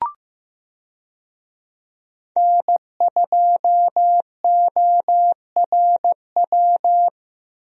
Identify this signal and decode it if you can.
Morse code, heard as a keyed tone, about 15 wpm, 700 Hz. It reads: N2ORW
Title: USA Call Signs - 5 Characters - Repeated 3x - 15wpm